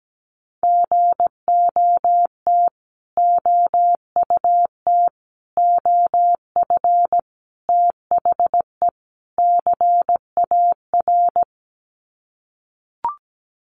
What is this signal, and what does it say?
Morse code, heard as a keyed tone, about 17 wpm, 700 Hz. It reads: GOT OUT OF THE CAR
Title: Sets of 5 Words Encore 17wpm